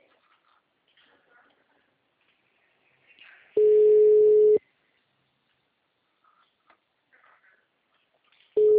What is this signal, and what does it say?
Ringback tone (Europe, 425 Hz).
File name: out- -2043- - - .10050